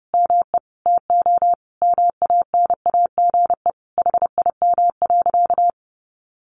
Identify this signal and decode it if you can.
Morse code, heard as a keyed tone, about 30 wpm, 700 Hz. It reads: ME TO MANAGE HIM.
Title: Sentences from Top 1400 Words 30wpm